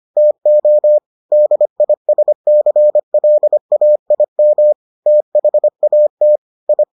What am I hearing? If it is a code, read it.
Morse code, heard as a keyed tone, about 25 wpm, 600 Hz. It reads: TO DISCLAIM THAT I